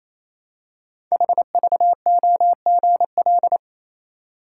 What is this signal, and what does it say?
Morse code, heard as a keyed tone, about 28 wpm, 700 Hz. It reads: HVOGL
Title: Code Groups - 5 Letter:Numbers - 28wpm